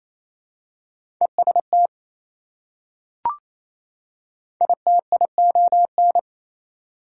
Morse code, heard as a keyed tone, about 28 wpm, 700 Hz. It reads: EST ITION
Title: Prefixes and Suffixes 28wpm